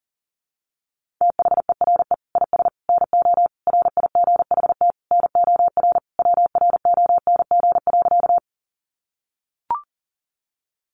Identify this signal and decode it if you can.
Morse code, heard as a keyed tone, about 40 wpm, 700 Hz. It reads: THERE IS NO RIGHT NOR WRONG.
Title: Sentences from Top 1500 Words 40wpm